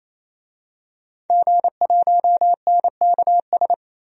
Morse code, heard as a keyed tone, about 28 wpm, 700 Hz. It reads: G1NKS
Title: DXCC-100 Call Signs - 5 Characters - Repeated 3x - 28wpm